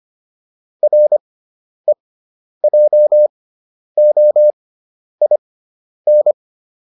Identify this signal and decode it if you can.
Morse code, heard as a keyed tone, about 25 wpm, spaced as for 10 wpm, 600 Hz. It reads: REJOIN